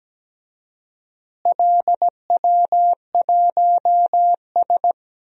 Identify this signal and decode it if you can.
Morse code, heard as a keyed tone, about 17 wpm, 700 Hz. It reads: LW1S
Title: DXCC-100 Call Signs - 4 Characters - Repeated 2x - 17wpm